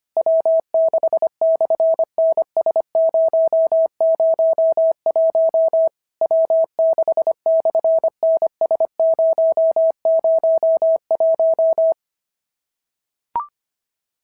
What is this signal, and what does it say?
Morse code, heard as a keyed tone, about 24 wpm, 650 Hz. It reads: W6/NS001 W6/NS001
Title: SOTA QSO Practice 25wpm